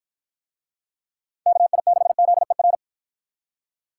Morse code, heard as a keyed tone, about 49 wpm, 700 Hz. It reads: KI6ZER